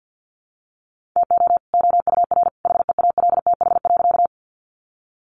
Morse code, heard as a keyed tone, about 50 wpm, 700 Hz. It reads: TO OUR HEALTH.
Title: Sentences from Top 600 Words 50wpm